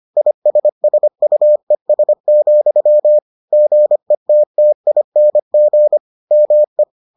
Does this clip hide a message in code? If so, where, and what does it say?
Yes: Morse code from 0.2 to 6.8 s, heard as a keyed tone, about 25 wpm, 600 Hz. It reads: ISSUES, GETTING ME